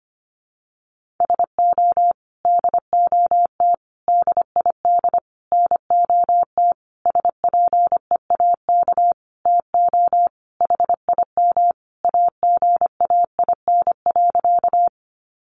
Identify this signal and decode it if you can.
Morse code, heard as a keyed tone, about 25 wpm, 700 Hz. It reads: SO DOT DID NOT SPEAK TO HIM AGAIN.
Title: Sentences from Top 1500 Words 25wpm